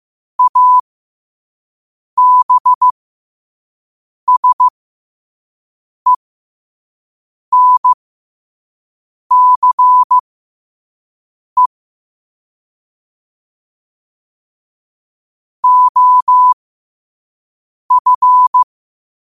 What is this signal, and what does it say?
Morse code, heard as a keyed tone, about 15 wpm, spaced as for 5 wpm, 1000 Hz. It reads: ABSENCE OF
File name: 5wpm